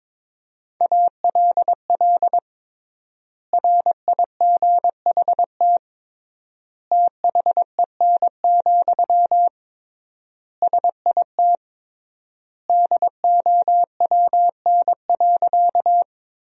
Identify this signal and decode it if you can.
Morse code, heard as a keyed tone, about 22 wpm, 700 Hz. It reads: ALL RIGHT THEN, SIT DOWN.
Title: 3x Word-Spacing - Sentences from Top 500 Words - 22wpm